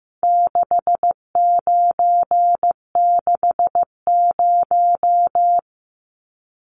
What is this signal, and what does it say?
Morse code, heard as a keyed tone, about 15 wpm, 700 Hz. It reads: 6960
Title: Numbers - 4-Digit 15wpm